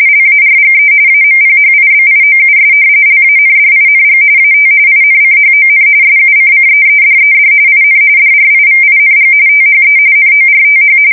RTTY, about 45.45 baud. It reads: LARGE AMPHIBIOUS LANDING AT INCHON SUCCEEDED IN DRIVING THE NORTH
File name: fsk170 / usnavy5